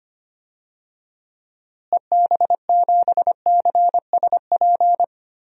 Morse code, heard as a keyed tone, about 25 wpm, 700 Hz. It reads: EB7CSP
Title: DXCC-100 Call Signs - 6 Characters - Repeated 3x - 25wpm